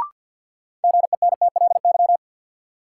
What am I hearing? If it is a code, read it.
Morse code, heard as a keyed tone, about 52 wpm, 700 Hz. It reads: GENTLY
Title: Sentences from Top 1500 Words - New Word Drill - 50wpm